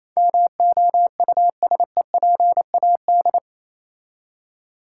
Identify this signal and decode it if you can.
Morse code, heard as a keyed tone, about 27 wpm, 700 Hz. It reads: MOUSEPAD
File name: Easy Words - Speed-Racing with Hints - 22wpm